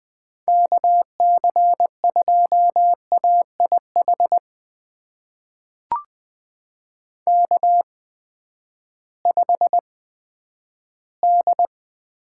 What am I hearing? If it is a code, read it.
Morse code, heard as a keyed tone, about 20 wpm, spaced as for 11 wpm, 700 Hz. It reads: KC2AIH K5D
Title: USA Call Signs - 8x Character Spacing - 20wpm